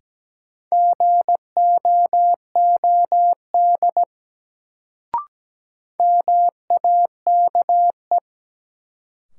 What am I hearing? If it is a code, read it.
Morse code, heard as a keyed tone, about 17 wpm, 700 Hz. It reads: GOOD MAKE
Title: Sets of 2 Words - New Word Drill - 17wpm